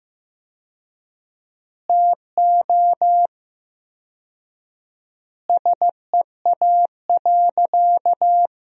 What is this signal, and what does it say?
Morse code, heard as a keyed tone, about 15 wpm, 700 Hz. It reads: TO SEA.
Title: 4x Word-Spacing - Sentences from Top 400 Words - 15wpm